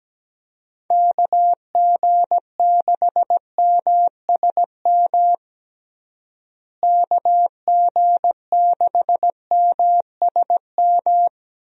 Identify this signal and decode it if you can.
Morse code, heard as a keyed tone, about 17 wpm, 700 Hz. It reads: KG6MSM KG6MSM
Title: USA Call Signs - 6 Characters - Repeated 3x - 17wpm